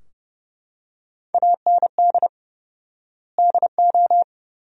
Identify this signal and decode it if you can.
Morse code, heard as a keyed tone, about 32 wpm, 700 Hz. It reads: AND DO
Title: 4x Word-Spacing - Sentences from Top 100 Words - 30wpm